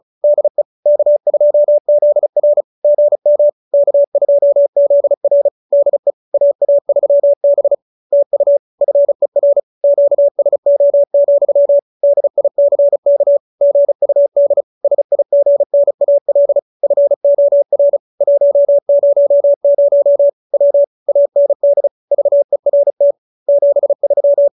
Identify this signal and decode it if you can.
Morse code, heard as a keyed tone, about 34 wpm, 570 Hz. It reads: DE K2ZR GM K2ZR DE AA3B TU FER QSO, DICK GUD SIGNAL FOR 100 W AND VERT 73
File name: QSO302 35